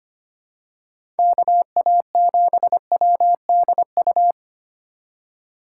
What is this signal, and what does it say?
Morse code, heard as a keyed tone, about 25 wpm, 700 Hz. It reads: KA7WDU
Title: USA Call Signs - 6 Characters - 3x Character Spacing - 25wpm